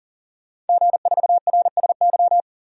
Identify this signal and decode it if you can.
Morse code, heard as a keyed tone, about 40 wpm, 700 Hz. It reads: G4RSY
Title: DXCC-100 Call Signs - 5 Characters - 40wpm